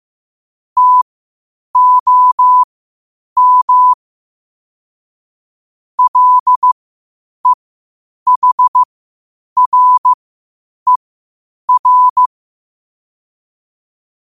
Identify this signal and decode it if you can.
Morse code, heard as a keyed tone, about 15 wpm, spaced as for 8 wpm, 1000 Hz. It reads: TOM LEHRER
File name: qotdfiles / 8wpm